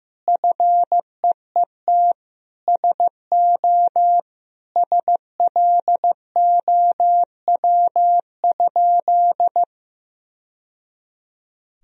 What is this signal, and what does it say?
Morse code, heard as a keyed tone, about 15 wpm, 700 Hz. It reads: FEET SO SLOW?
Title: Sentences from Top 1000 Words - Review Part 2 - 15wpm